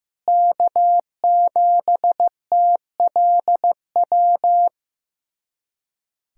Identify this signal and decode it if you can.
Morse code, heard as a keyed tone, about 15 wpm, 700 Hz. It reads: K7TLW
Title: USA Callsign DE Callsign - 5 Characters - 15wpm